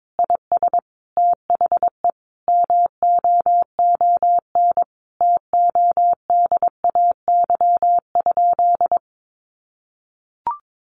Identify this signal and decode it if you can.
Morse code, heard as a keyed tone, about 22 wpm, 700 Hz. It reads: IS THE MOON TODAY?
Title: Sentences from Top 1200 Words 22wpm